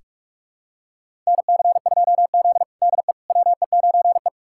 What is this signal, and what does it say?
Morse code, heard as a keyed tone, about 47 wpm, 700 Hz. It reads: NC2Z DE WE9E
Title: USA Callsign DE Callsign - 4 Characters - 45wpm